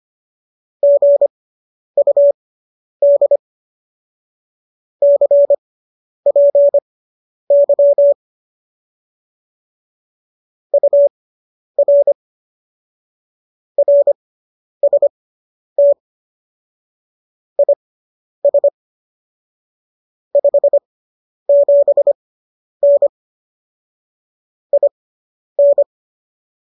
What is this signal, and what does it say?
Morse code, heard as a keyed tone, about 25 wpm, spaced as for 10 wpm, 570 Hz. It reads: GUD CPY UR RST IS 57N IN
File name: qso219 10